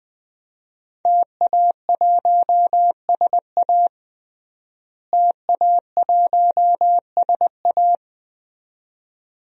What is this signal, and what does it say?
Morse code, heard as a keyed tone, about 20 wpm, 700 Hz. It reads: TA1SA TA1SA
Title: DXCC-100 Call Signs - 5 Characters - Repeated 2x - 20wpm